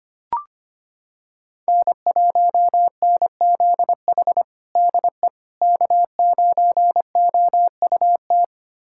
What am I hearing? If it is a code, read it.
Morse code, heard as a keyed tone, about 25 wpm, 700 Hz. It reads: N1NZH DE K9OUT
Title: USA Callsign DE Callsign - 5 Characters - 25wpm